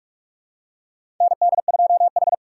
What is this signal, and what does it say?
Morse code, heard as a keyed tone, about 46 wpm, 700 Hz. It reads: ND2H
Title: USA Call Signs - 4 Characters - 4x Character Spacing - 45wpm